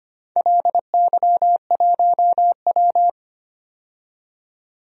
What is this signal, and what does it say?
Morse code, heard as a keyed tone, about 25 wpm, 700 Hz. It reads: LY1W
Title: DXCC-100 Call Signs - 4 Characters - 25wpm